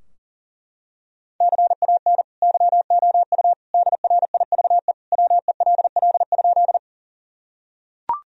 Morse code, heard as a keyed tone, about 40 wpm, 700 Hz. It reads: CAN YOU DRIVE WELL?
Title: Sentences from Top 1000 Words - Review Part 2 - 40wpm